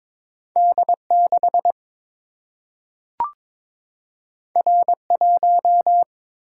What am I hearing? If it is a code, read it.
Morse code, heard as a keyed tone, about 22 wpm, 700 Hz. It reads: D6 R1
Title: Single-Letter Single-Number 22wpm